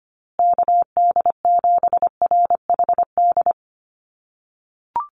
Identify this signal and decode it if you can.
Morse code, heard as a keyed tone, about 25 wpm, 700 Hz. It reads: KD7RHD
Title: USA Call Signs Repeated 3x 25wpm